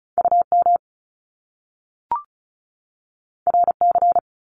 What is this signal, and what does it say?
Morse code, heard as a keyed tone, about 36 wpm, 700 Hz. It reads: UM RC
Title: Top Two Letter Combinations in Words 35wpm